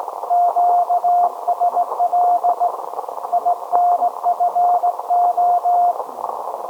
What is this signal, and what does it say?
Morse code, heard as a keyed tone, about 18 wpm, 710 Hz. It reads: QSL INFO
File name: storage / message Grimeton SAQ